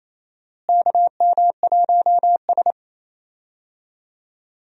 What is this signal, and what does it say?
Morse code, heard as a keyed tone, about 28 wpm, 700 Hz. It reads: KM1S
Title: USA Call Signs - 4 Characters - Repeated 3x - 28wpm